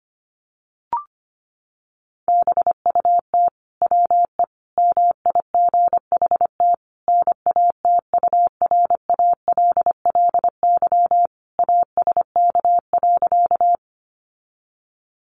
Morse code, heard as a keyed tone, about 25 wpm, 700 Hz. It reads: BUT WE MIGHT NATURALLY ASK.
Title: Sentences from Top 1300 Words 25wpm